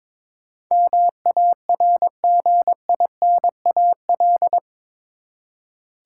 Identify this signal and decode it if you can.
Morse code, heard as a keyed tone, about 22 wpm, 700 Hz. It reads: MARGINAL
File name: practice / 22wpm / Eight Letter Words - 22wpm